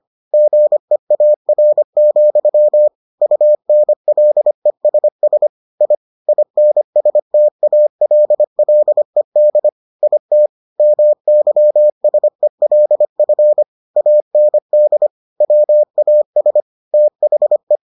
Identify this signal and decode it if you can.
Morse code, heard as a keyed tone, about 25 wpm, 600 Hz. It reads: GEAR, UNLESS I INSTALLED IT MYSELF AND WAS THE